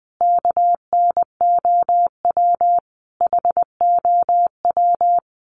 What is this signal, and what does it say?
Morse code, heard as a keyed tone, about 20 wpm, 700 Hz. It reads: KNOW HOW
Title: Sets of 3 Words 20wpm